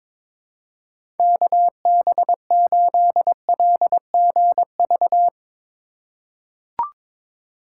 Morse code, heard as a keyed tone, about 22 wpm, 700 Hz. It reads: KB8LGV
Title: USA Call Signs - 6 Characters - 2x Character Spacing - 22wpm